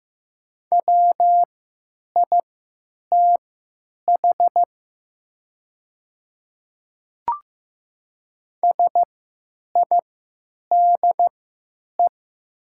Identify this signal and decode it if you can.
Morse code, heard as a keyed tone, about 15 wpm, spaced as for 9 wpm, 700 Hz. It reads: WITH SIDE
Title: 3x Character Spacing - Top 200 Words - 15wpm